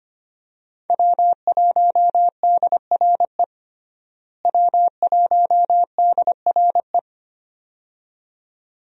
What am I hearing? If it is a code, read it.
Morse code, heard as a keyed tone, about 25 wpm, 700 Hz. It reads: W1DRE W1DRE
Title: USA Call Signs - 5 Characters - Repeated 2x - 25wpm